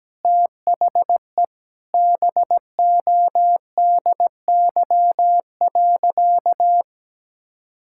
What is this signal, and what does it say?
Morse code, heard as a keyed tone, about 17 wpm, 700 Hz. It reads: THE BODY.
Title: Sentences from Top 1000 Words - Review Part 6 - 17wpm